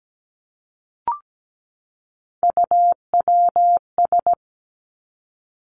Morse code, heard as a keyed tone, about 17 wpm, 700 Hz. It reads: UWS
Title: Three Challenging Letter-Number Combinations 17wpm